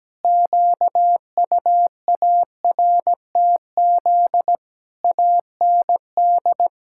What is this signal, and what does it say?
Morse code, heard as a keyed tone, about 17 wpm, 700 Hz. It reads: QUARTZ AND
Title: Pangram Challenge v2 - 17wpm